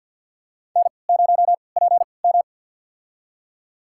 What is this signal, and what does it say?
Morse code, heard as a keyed tone, about 53 wpm, spaced as for 29 wpm, 700 Hz. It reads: N0PM
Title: USA Call Signs - 4 Characters - 3x Character Spacing - 50wpm